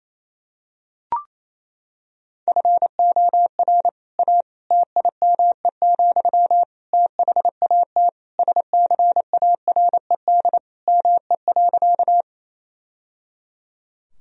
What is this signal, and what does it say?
Morse code, heard as a keyed tone, about 27 wpm, 700 Hz. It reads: FOR A TIME, THAT SCARED ME.
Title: Sentences from Top 1000 Words - Review Part 4 - 28wpm